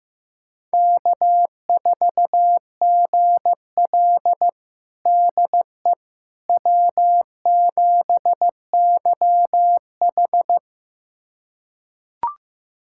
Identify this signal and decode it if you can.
Morse code, heard as a keyed tone, about 15 wpm, 700 Hz. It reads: K4GL DE W7YH
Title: USA Callsign DE Callsign - 4 Characters - 15wpm